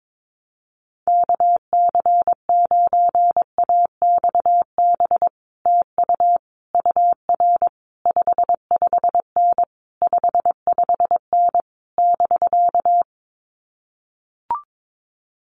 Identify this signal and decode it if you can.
Morse code, heard as a keyed tone, about 22 wpm, 700 Hz. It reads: KC9AXB TU UR 55N 55N <BK>
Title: Common QSO Phrases Encore 22wpm